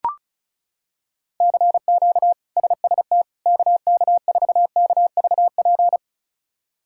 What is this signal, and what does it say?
Morse code, heard as a keyed tone, about 36 wpm, 700 Hz. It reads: CQ SST KK4KVP
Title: K1USN SST QSO Practice - 35wpm